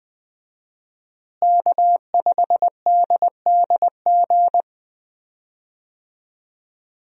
Morse code, heard as a keyed tone, about 20 wpm, 700 Hz. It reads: K5DDG
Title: USA Call Signs - 5 Characters - Repeated 2x - 20wpm